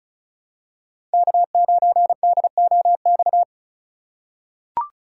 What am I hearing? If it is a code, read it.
Morse code, heard as a keyed tone, about 36 wpm, 700 Hz. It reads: K9DOX
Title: USA Call Signs - 5 Characters - 4x Character Spacing - 35wpm